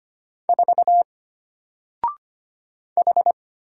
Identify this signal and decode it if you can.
Morse code, heard as a keyed tone, about 25 wpm, 700 Hz. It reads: 4 H
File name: Single-Letter-Number-25wpm